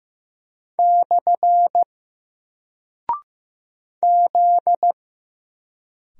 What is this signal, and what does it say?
Morse code, heard as a keyed tone, about 15 wpm, 700 Hz. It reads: / Z
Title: Single Character (Letter:Number:Punctuation:Prosign) 15wpm